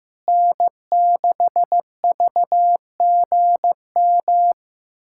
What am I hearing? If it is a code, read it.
Morse code, heard as a keyed tone, about 16 wpm, 700 Hz. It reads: N6VGM
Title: USA Callsign DE Callsign - 5 Characters - 15wpm